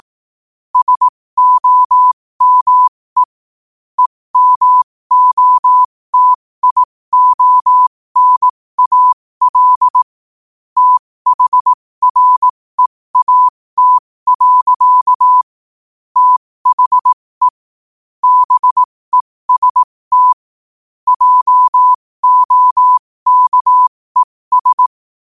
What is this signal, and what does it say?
Morse code, heard as a keyed tone, about 18 wpm, spaced as for 15 wpm, 1000 Hz. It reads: SOME EMOTIONAL THREAT. THE BEST JOKES
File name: qotdfiles / 15wpm